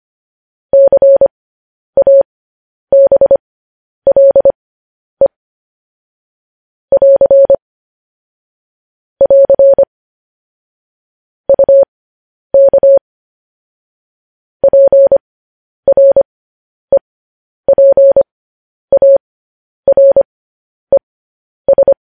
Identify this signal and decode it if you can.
Morse code, heard as a keyed tone, about 25 wpm, spaced as for 10 wpm, 550 Hz. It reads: CABLE <AR> <AR> UK PREPARES
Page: News Headlines in Morse Code at 10 WPM